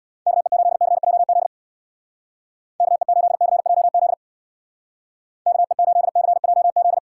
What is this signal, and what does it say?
Morse code, heard as a keyed tone, about 62 wpm, 700 Hz. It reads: KE8CJZ KE8CJZ KE8CJZ
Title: Speed Racing - USA 2x3 Call Signs 50wpm